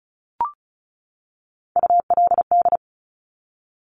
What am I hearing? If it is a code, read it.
Morse code, heard as a keyed tone, about 36 wpm, 700 Hz. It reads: ULD
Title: Top Three Letter Combinations in Words 35wpm